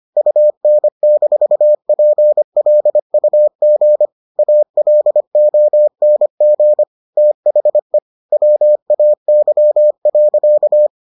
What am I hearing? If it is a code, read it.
Morse code, heard as a keyed tone, about 25 wpm, 600 Hz. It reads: UN-PLUG ALONG THE WAY.